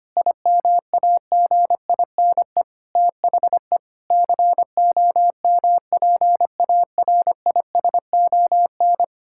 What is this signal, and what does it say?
Morse code, heard as a keyed tone, about 25 wpm, 700 Hz. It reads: IMAGINE THE COMPARISON
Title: Sentences from Top 1600 Words 25wpm